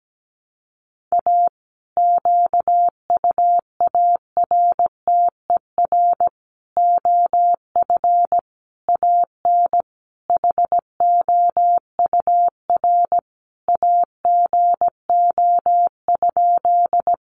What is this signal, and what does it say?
Morse code, heard as a keyed tone, about 17 wpm, 700 Hz. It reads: A QUARTER OF AN HOUR AGO?
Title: Sentences from Top 1300 Words 17wpm